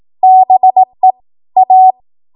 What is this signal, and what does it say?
Morse code, heard as a keyed tone, about 18 wpm, 750 Hz. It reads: BE A
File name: Morse / 18WPM